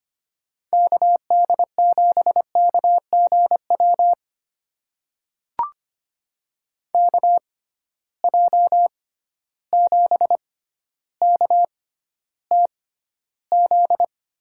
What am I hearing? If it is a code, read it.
Morse code, heard as a keyed tone, about 25 wpm, spaced as for 13 wpm, 700 Hz. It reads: KD7KGW KJ7KTZ
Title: USA Call Signs - 6 Characters - 6x Character Spacing - 25wpm